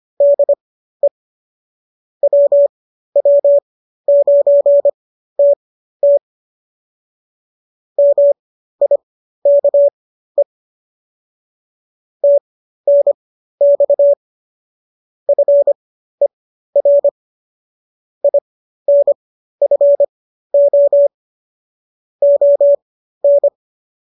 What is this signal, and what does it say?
Morse code, heard as a keyed tone, about 25 wpm, spaced as for 13 wpm, 570 Hz. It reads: DE WW9TT MIKE TNX FER INFO ON